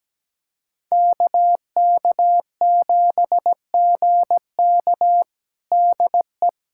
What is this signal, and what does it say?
Morse code, heard as a keyed tone, about 17 wpm, 700 Hz. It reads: KK7GK DE
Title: USA Callsign DE Callsign - 5 Characters - 17wpm